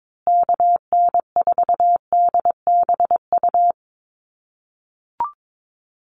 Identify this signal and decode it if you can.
Morse code, heard as a keyed tone, about 22 wpm, 700 Hz. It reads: KN4DBU
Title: USA Call Signs Repeated 3x 22wpm